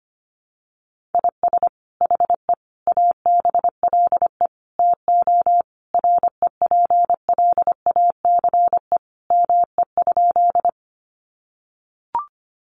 Morse code, heard as a keyed tone, about 25 wpm, 700 Hz. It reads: IS HE ABLE TO REPLACE ME?
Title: Sentences from Top 1400 Words 25wpm